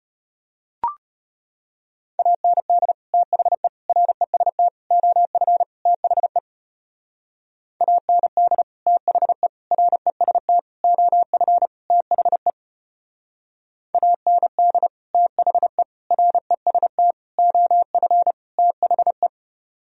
Morse code, heard as a keyed tone, about 33 wpm, 700 Hz. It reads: AND THE REST OF THE AND THE REST OF THE AND THE REST OF THE